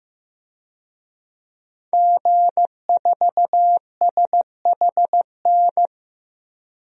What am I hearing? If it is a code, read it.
Morse code, heard as a keyed tone, about 15 wpm, 700 Hz. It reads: G4SHN
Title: DXCC-100 Call Signs - 5 Characters - Repeated 2x - 15wpm